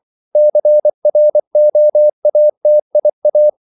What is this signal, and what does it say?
Morse code, heard as a keyed tone, about 24 wpm, 600 Hz. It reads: CROATIA
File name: CW / Practice / NationalCapitals24